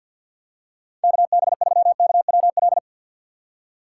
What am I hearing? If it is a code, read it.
Morse code, heard as a keyed tone, about 52 wpm, 700 Hz. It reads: KB3KWL
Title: USA Call Signs - 4x Character Spacing - 50wpm